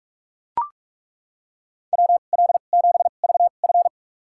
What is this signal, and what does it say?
Morse code, heard as a keyed tone, about 42 wpm, spaced as for 31 wpm, 700 Hz. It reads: WL7VF
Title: USA Call Signs - 5 Characters - 2x Character Spacing - 45wpm